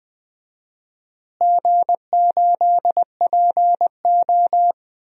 Morse code, heard as a keyed tone, about 20 wpm, 700 Hz. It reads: G8PO
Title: DXCC-100 Call Signs - 4 Characters - Repeated 3x - 20wpm